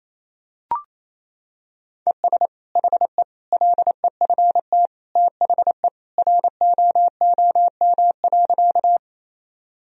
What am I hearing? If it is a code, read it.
Morse code, heard as a keyed tone, about 28 wpm, 700 Hz. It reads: ES HE LEFT THE ROOM.
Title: Sentences from Top 300 Words and Abbreviations 28wpm